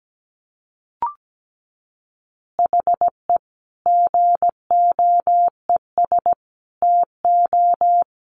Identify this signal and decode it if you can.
Morse code, heard as a keyed tone, about 17 wpm, 700 Hz. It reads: HE GOES TO
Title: Additional Easy Phrases 17wpm